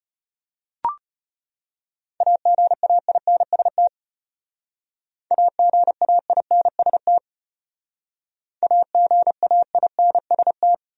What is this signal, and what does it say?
Morse code, heard as a keyed tone, about 31 wpm, 700 Hz. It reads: AGAINST AGAINST AGAINST
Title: Speed-Racing Top 500 Words 25wpm